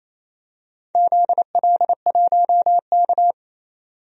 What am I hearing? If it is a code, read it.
Morse code, heard as a keyed tone, about 28 wpm, 700 Hz. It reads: ZL1K
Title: DXCC-100 Call Signs - 4 Characters - 28wpm